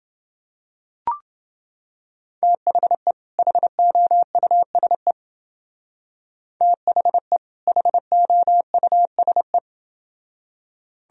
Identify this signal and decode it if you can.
Morse code, heard as a keyed tone, about 28 wpm, 700 Hz. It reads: THE HOUSE THE HOUSE